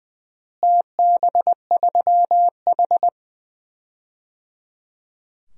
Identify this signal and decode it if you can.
Morse code, heard as a keyed tone, about 20 wpm, 700 Hz. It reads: TB3H
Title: DXCC-100 Call Signs - 4 Characters - Repeated 3x - 20wpm